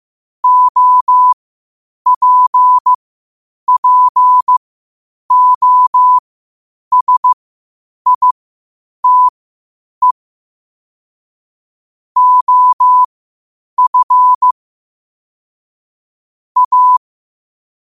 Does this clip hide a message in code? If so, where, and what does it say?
Yes: Morse code from 0.4 to 17.0 s, heard as a keyed tone, about 15 wpm, spaced as for 8 wpm, 1000 Hz. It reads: OPPOSITE OF A